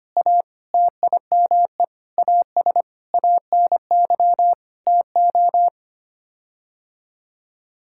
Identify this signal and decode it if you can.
Morse code, heard as a keyed tone, about 25 wpm, 700 Hz. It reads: A TIME AS ANY TO
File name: Sets of 9 Words - 25wpm